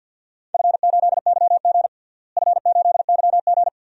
Morse code, heard as a keyed tone, about 50 wpm, 700 Hz. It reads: F8YG F8YG
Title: DXCC-100 Call Signs - 4 Characters - Repeated 2x - 50wpm